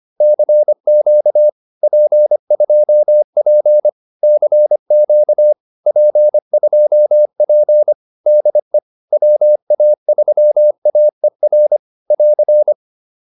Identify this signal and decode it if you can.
Morse code, heard as a keyed tone, about 25 wpm, 600 Hz. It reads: CQ P2P CQ P2P DE WA3AER <AR>
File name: POTA306a 25